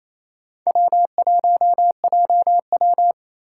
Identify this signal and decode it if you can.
Morse code, heard as a keyed tone, about 28 wpm, 700 Hz. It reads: W1JW
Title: USA Call Signs - 4 Characters - 2x Character Spacing - 28wpm